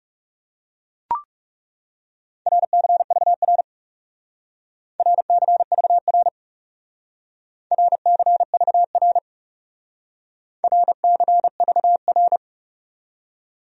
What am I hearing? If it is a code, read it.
Morse code, heard as a keyed tone, about 40 wpm, 700 Hz. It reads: RCVR RCVR RCVR RCVR